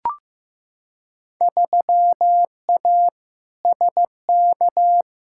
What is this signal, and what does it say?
Morse code, heard as a keyed tone, about 15 wpm, 700 Hz. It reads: 3A SK
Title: Field Day Exchanges Encore - 15wpm